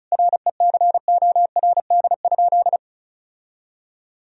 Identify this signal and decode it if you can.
Morse code, heard as a keyed tone, about 36 wpm, 700 Hz. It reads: RECORD?
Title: Sentences from Top 1000 Words - Review Part 4 - 35wpm